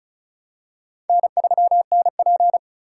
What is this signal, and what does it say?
Morse code, heard as a keyed tone, about 36 wpm, 700 Hz. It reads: N3NP
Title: USA Call Signs - 4 Characters - 2x Character Spacing - 35wpm